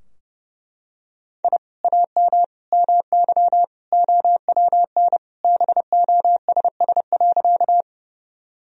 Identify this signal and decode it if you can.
Morse code, heard as a keyed tone, about 30 wpm, 700 Hz. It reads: I AM MY OWN BOSS.
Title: Sentences from Top 1000 Words - Review Part 4 - 30wpm